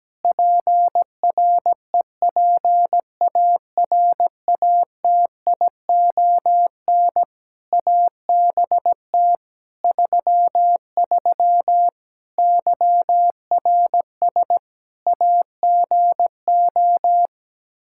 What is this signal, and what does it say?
Morse code, heard as a keyed tone, about 17 wpm, 700 Hz. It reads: PREPARATION ABT 33 YRS AGO
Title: K1USN SST QSO Practice 17wpm